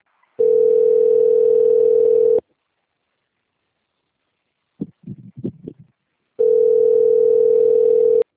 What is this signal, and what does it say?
Ringback tone (North America), 440 + 480 Hz.
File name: out- - - - - .24185